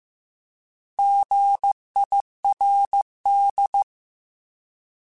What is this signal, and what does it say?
Morse code, heard as a keyed tone, about 15 wpm, 800 Hz. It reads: GIRD